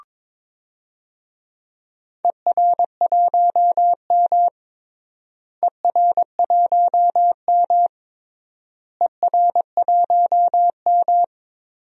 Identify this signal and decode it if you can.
Morse code, heard as a keyed tone, about 22 wpm, 700 Hz. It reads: ER1M ER1M ER1M
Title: DXCC-100 Call Signs - 4 Characters - Repeated 3x - 22wpm